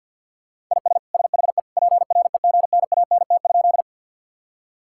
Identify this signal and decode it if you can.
Morse code, heard as a keyed tone, about 52 wpm, 700 Hz. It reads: IS SHE PREGNANT?
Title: Sentences from Top 1000 Words - Review Part 6 - 50wpm